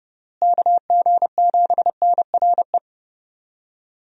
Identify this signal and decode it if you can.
Morse code, heard as a keyed tone, about 32 wpm, 700 Hz. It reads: KG7NRE
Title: USA Call Signs - 6 Characters - Repeated 3x - 30wpm